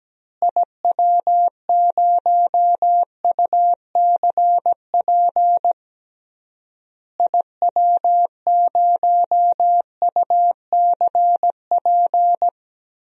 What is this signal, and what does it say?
Morse code, heard as a keyed tone, about 17 wpm, 700 Hz. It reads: IW0UCP IW0UCP
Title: DXCC-100 Call Signs - 6 Characters - Repeated 3x - 17wpm